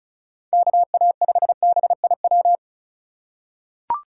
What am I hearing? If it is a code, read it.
Morse code, heard as a keyed tone, about 36 wpm, 700 Hz. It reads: KA5BIW
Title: USA Call Signs - 6 Characters - 3x Character Spacing - 35wpm